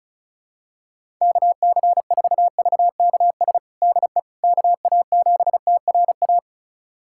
Morse code, heard as a keyed tone, about 36 wpm, 700 Hz. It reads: KC4VKS DE KA7TRA
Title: USA Callsign DE Callsign 35wpm